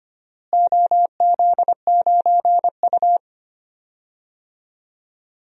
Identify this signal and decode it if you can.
Morse code, heard as a keyed tone, about 25 wpm, 700 Hz. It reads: OZ9U
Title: DXCC-100 Call Signs - 4 Characters - Repeated 2x - 25wpm